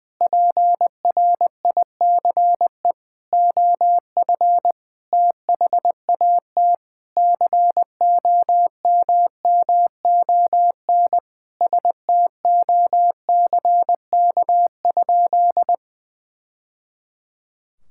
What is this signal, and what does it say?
Morse code, heard as a keyed tone, about 20 wpm, 700 Hz. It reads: PRICE OF THAT COMMON STOCK?
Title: Sentences from Top 1000 Words - Review Part 5 - 20wpm